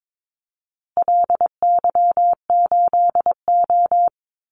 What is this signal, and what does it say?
Morse code, heard as a keyed tone, about 22 wpm, 700 Hz. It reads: LY8O
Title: Call Signs Repeated 2x - 22wpm